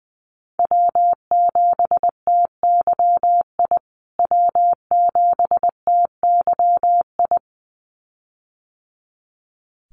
Morse code, heard as a keyed tone, about 20 wpm, 700 Hz. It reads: W7TYI W7TYI
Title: USA Call Signs Repeated 2x 20wpm